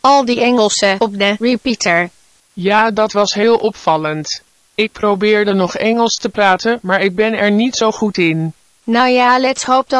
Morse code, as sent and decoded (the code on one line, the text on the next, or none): none